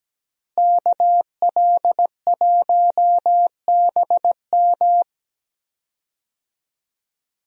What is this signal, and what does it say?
Morse code, heard as a keyed tone, about 17 wpm, 700 Hz. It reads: KL1BM
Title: USA Call Signs - 5 Characters - Repeated 3x - 17wpm